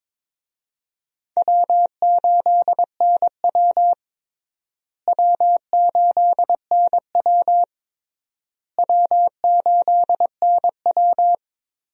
Morse code, heard as a keyed tone, about 22 wpm, 700 Hz. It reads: W8NW W8NW W8NW
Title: USA Call Signs - 4 Characters - Repeated 3x - 22wpm